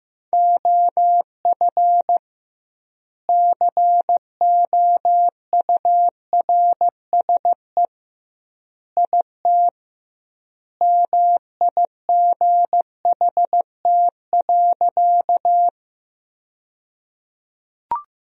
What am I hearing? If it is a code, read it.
Morse code, heard as a keyed tone, about 15 wpm, 700 Hz. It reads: OF COURSE IT MIGHT.
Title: 2x Word-Spacing - Sentences from Top 500 Words - 15wpm